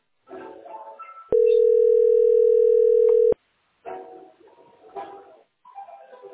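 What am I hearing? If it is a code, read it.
Ringback tone (North America), 440 + 480 Hz.
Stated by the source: Telephone call recorded on the tesspb.mdpbx PBX.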